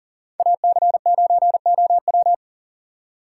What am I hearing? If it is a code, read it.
Morse code, heard as a keyed tone, about 40 wpm, 700 Hz. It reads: AC9OW
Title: USA Call Signs - 5 Characters - 6x Character Spacing - 40wpm